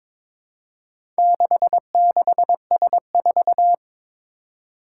Morse code, heard as a keyed tone, about 22 wpm, 700 Hz. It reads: 66S4
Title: Four Challenging Letter-Number Combinations 22wpm